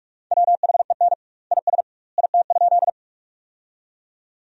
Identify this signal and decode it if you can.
Morse code, heard as a keyed tone, about 45 wpm, 700 Hz. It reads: WHEN IS IT?
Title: 2x Word-Spacing - Sentences from Top 100 Words - 45wpm